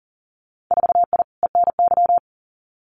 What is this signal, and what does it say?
Morse code, heard as a keyed tone, about 39 wpm, 700 Hz. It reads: 4I ENY
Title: Winter Field Day Exchanges Encore 40wpm